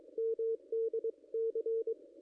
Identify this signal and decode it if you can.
Morse code, heard as a keyed tone, about 20 wpm, 450 Hz, with radio noise. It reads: MDC
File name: contests / ss-cw-k5tr-op-n5rz -0200